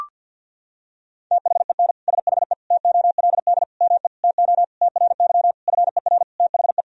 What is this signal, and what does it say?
Morse code, heard as a keyed tone, about 51 wpm, 700 Hz. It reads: THEN SHE TOLD ME TO TRY FER THE
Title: Sentences from Top 400 Words and Abbreviations 50wpm